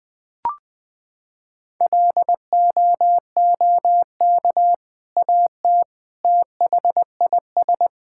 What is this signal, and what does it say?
Morse code, heard as a keyed tone, about 21 wpm, 700 Hz. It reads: LOOK AT THIS
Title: Sentences with 2-Word Phrases - 20wpm